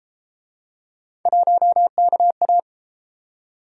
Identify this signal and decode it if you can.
Morse code, heard as a keyed tone, about 33 wpm, 700 Hz. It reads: 1KA